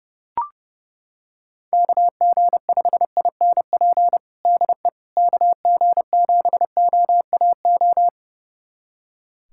Morse code, heard as a keyed tone, about 30 wpm, 700 Hz. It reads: KG5INP DE KG7OAO
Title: USA Callsign DE Callsign - 6 Characters - 30wpm